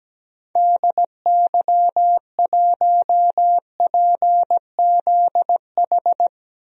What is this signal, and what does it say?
Morse code, heard as a keyed tone, about 17 wpm, 700 Hz. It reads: DY1PZH
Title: DXCC-100 Call Signs - 6 Characters - Repeated 3x - 17wpm